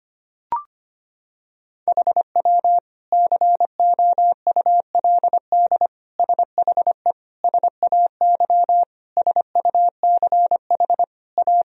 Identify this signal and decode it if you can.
Morse code, heard as a keyed tone, about 25 wpm, 700 Hz. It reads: HW COULD SHE SAY SUCH A
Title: Sentences from Top 200 Words and Abbreviations 25wpm